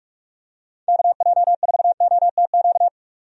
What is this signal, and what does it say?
Morse code, heard as a keyed tone, about 46 wpm, 700 Hz. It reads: KJ4OTQ
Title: USA Call Signs - 6 Characters - 45wpm